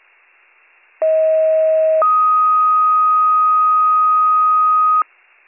Two-tone page, heard as a one-second tone, then a three-second tone, about 640 then 1250 Hz. It is heard from 1.0 to 5.0 s.